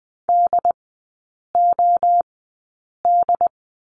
Morse code, heard as a keyed tone, about 20 wpm, spaced as for 8 wpm, 700 Hz. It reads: DOD
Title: 093 Lesson 16e - Letter c - Three Characters 20wpm